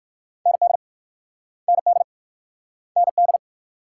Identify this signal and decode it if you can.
Morse code, heard as a keyed tone, about 53 wpm, 690 Hz. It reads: ND ND ND
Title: Speed-Racing Multi-letter Phonemes - 45wpm